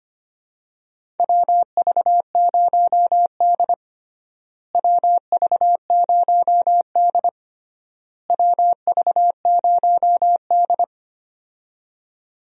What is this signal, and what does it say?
Morse code, heard as a keyed tone, about 25 wpm, 700 Hz. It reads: WV0D WV0D WV0D
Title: USA Call Signs - 4 Characters - Repeated 3x - 25wpm